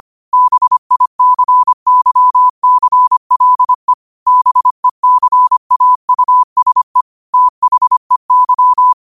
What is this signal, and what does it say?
Morse code, heard as a keyed tone, about 25 wpm, 1000 Hz. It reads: BICYCLE BECAUSE THEY